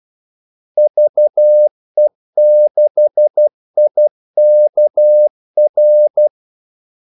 Morse code, heard as a keyed tone, about 12 wpm, 600 Hz. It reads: VE6IKR
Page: 2 ekstra ordmellemrum mellem kaldesignalerne